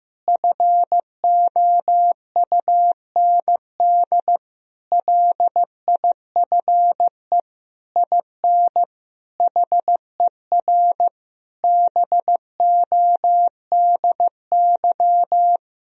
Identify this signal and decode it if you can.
Morse code, heard as a keyed tone, about 15 wpm, 700 Hz. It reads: FOUND LIFE IN HER BODY
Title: Sentences from Top 1000 Words - Review Part 2 - 15wpm